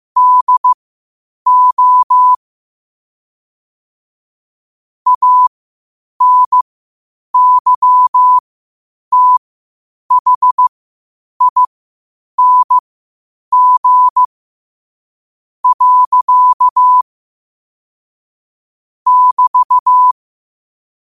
Morse code, heard as a keyed tone, about 15 wpm, spaced as for 8 wpm, 1000 Hz. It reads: DO ANYTHING. <BT>
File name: qotdfiles / 8wpm